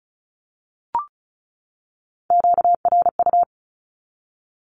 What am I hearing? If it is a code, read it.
Morse code, heard as a keyed tone, about 34 wpm, 700 Hz. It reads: QRU
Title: Q-Codes 35wpm